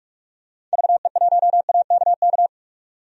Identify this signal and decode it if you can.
Morse code, heard as a keyed tone, about 48 wpm, 700 Hz. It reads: VE1AKK
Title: DXCC-100 Call Signs - 6 Characters - 45wpm